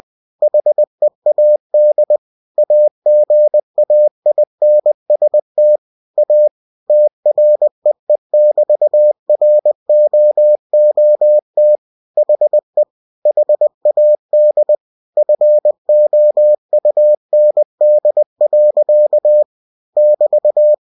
Morse code, heard as a keyed tone, about 20 wpm, 600 Hz. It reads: HEAD AGAINST A TREE<BT>ROOT HE HAD FOUND. <BT>
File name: cw-books / winnie-the-pooh / winnie-20wpm-page-0043